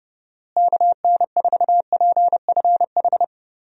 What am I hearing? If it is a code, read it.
Morse code, heard as a keyed tone, about 30 wpm, 700 Hz. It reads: KN4PFH
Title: USA Call Signs - 6 Characters - Repeated 3x - 30wpm